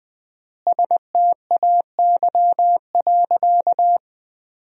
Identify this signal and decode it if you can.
Morse code, heard as a keyed tone, about 20 wpm, 700 Hz. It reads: STAY.
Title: 4x Word-Spacing - Sentences from Top 500 Words - 20wpm